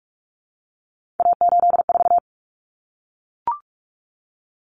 Morse code, heard as a keyed tone, about 44 wpm, 700 Hz. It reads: A84
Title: Common Cut Numbers - 3-Digit 45wpm